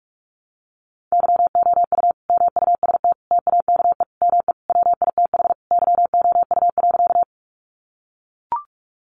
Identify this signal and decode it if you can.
Morse code, heard as a keyed tone, about 44 wpm, 700 Hz. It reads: YOU MUST TAKE ME WITH YOU.
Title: Sentences from Top 200 Words 45wpm